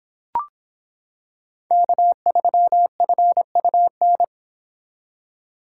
Morse code, heard as a keyed tone, about 25 wpm, 700 Hz. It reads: K3FUN